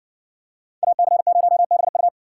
Morse code, heard as a keyed tone, about 55 wpm, 700 Hz. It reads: AC0BU